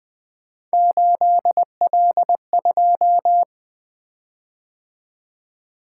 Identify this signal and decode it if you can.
Morse code, heard as a keyed tone, about 20 wpm, 700 Hz. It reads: 8L2
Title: Three Letter-Number Combinations - 20wpm